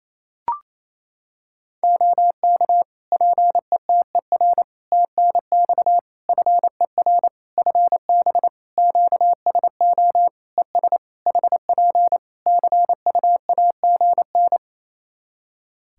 Morse code, heard as a keyed tone, about 28 wpm, 700 Hz. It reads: OK PETER TNX FER FB QSO ES HP CUAGN
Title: Practice QSO - Exchange 4 - Ending - 28wpm